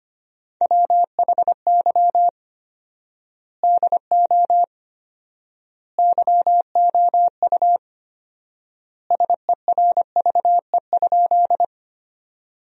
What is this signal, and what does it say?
Morse code, heard as a keyed tone, about 25 wpm, 700 Hz. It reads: WHY DO YOU SERVE?
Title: 4x Word-Spacing - Sentences from Top 400 Words - 25wpm